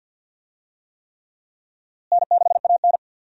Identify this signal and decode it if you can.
Morse code, heard as a keyed tone, about 49 wpm, 700 Hz. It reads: N6AN
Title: USA Call Signs - 4 Characters - Repeated 3x - 50wpm